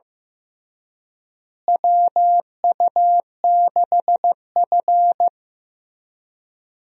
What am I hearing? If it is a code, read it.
Morse code, heard as a keyed tone, about 15 wpm, 700 Hz. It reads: WU6F
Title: USA Call Signs - 4 Characters - Repeated 3x - 15wpm